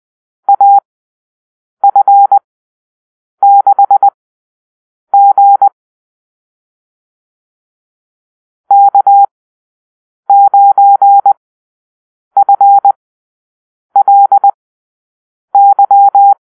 Morse code, heard as a keyed tone, about 20 wpm, spaced as for 7 wpm, 800 Hz. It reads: AF6G K9FLY
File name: arrl / Call-Sign Drill-7